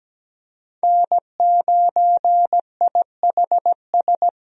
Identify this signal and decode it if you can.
Morse code, heard as a keyed tone, about 17 wpm, 700 Hz. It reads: N9IHS
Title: USA Call Signs - 5 Characters - Repeated 3x - 17wpm